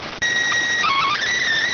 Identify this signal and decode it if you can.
SSTV (slow-scan television), PD 290.